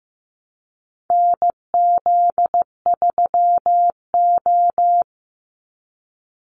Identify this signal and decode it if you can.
Morse code, heard as a keyed tone, about 15 wpm, 700 Hz. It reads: NZ3O
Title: Call Signs Repeated 2x - 15wpm